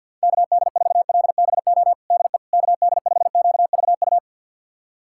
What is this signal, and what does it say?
Morse code, heard as a keyed tone, about 50 wpm, 700 Hz. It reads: KD4LBO DE KD5QVU
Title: USA Callsign DE Callsign - 6 Characters - 50wpm